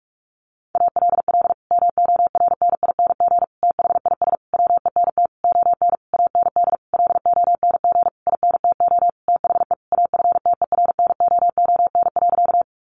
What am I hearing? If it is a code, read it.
Morse code, heard as a keyed tone, about 46 wpm, 700 Hz. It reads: ALL MORNING THIS WENT ON AND LONG INTO THE AFTERNOON.
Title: Sentences from Top 1500 Words 45wpm